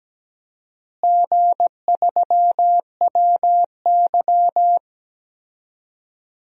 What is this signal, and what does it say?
Morse code, heard as a keyed tone, about 17 wpm, 700 Hz. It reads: G3WY
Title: DXCC-100 Call Signs - 4 Characters - Repeated 3x - 17wpm